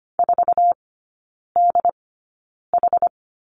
Morse code, heard as a keyed tone, about 25 wpm, spaced as for 9 wpm, 690 Hz. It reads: 4DH
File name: Lesson 14e - Number 2 - Three Characters 25wpm